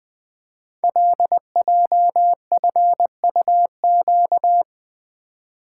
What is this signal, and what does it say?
Morse code, heard as a keyed tone, about 20 wpm, 700 Hz. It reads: LJFUQ
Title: Code Groups - 5 Letters - 20wpm